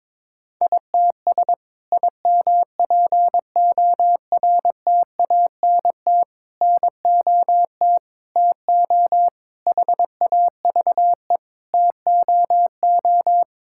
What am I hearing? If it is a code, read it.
Morse code, heard as a keyed tone, about 21 wpm, 700 Hz. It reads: ITS IMPORTANT NOT TO HAVE TOO
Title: Sentences from Top 1000 Words - Review Part 5 - 22wpm